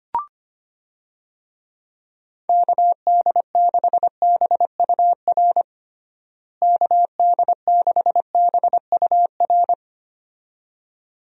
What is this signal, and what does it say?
Morse code, heard as a keyed tone, about 25 wpm, 700 Hz. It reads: KD6BUR KD6BUR
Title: USA Call Signs - 6 Characters - Repeated 2x - 25wpm